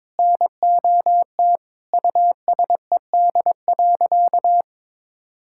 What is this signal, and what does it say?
Morse code, heard as a keyed tone, about 22 wpm, 700 Hz. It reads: NOT USED.
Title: Sentences from Top 1600 Words 22wpm